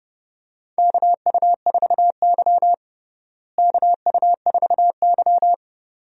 Morse code, heard as a keyed tone, about 31 wpm, 700 Hz. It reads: KU4Y KU4Y
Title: USA Call Signs - 4 Characters - Repeated 2x - 30wpm